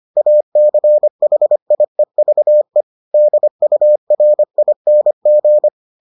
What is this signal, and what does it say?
Morse code, heard as a keyed tone, about 25 wpm, 600 Hz. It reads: ACHIEVE DURING